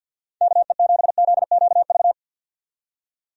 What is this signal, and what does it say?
Morse code, heard as a keyed tone, about 52 wpm, 700 Hz. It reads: KE7ZQV
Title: USA Call Signs - 6 Characters - 50wpm